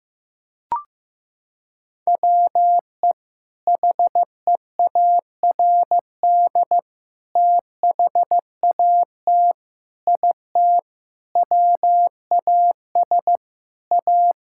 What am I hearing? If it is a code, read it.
Morse code, heard as a keyed tone, about 15 wpm, 700 Hz. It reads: WE HEARD THAT IT WAS A
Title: Sentences from Top 1000 Words - Review Part 6 - 15wpm